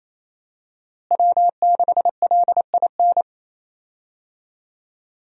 Morse code, heard as a keyed tone, about 28 wpm, 700 Hz. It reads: W6LIN
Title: USA Call Signs - 5 Characters - Repeated 2x - 28wpm